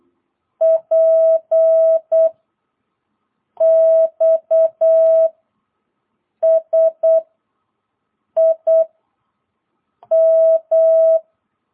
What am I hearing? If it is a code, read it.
Morse code, heard as a keyed tone, about 8 wpm, 660 Hz. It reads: P X S I M
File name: jh8cbh / 2022cw